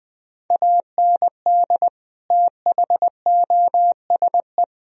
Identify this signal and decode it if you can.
Morse code, heard as a keyed tone, about 20 wpm, 700 Hz. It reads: AND THOSE
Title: Sentences from Top 1000 Words - Review Part 1 - 20wpm